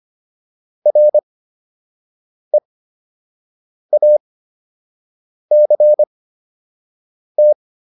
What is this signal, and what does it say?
Morse code, heard as a keyed tone, about 25 wpm, spaced as for 6 wpm, 600 Hz. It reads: REACT